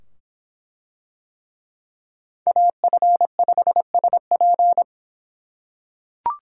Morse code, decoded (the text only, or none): AF5SP